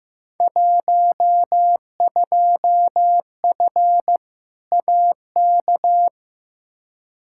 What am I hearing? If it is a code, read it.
Morse code, heard as a keyed tone, about 15 wpm, 700 Hz. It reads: 12F AK
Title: Field Day Exchanges Encore - 15wpm